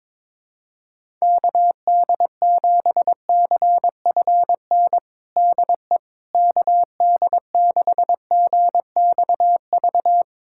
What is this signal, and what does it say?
Morse code, heard as a keyed tone, about 22 wpm, 700 Hz. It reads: KD7CFN DE KD6GXV
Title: USA Callsign DE Callsign 22wpm